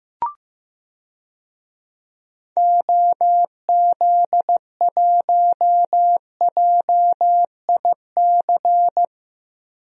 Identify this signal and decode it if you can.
Morse code, heard as a keyed tone, about 15 wpm, 700 Hz. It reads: OZ1JIC
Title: DXCC-100 Call Signs - 6 Characters - 15wpm